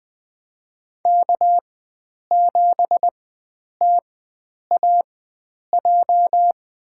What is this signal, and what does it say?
Morse code, heard as a keyed tone, about 21 wpm, spaced as for 9 wpm, 700 Hz. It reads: K7TAJ
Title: USA Call Signs - 5 Characters - 4x Character Spacing - 20wpm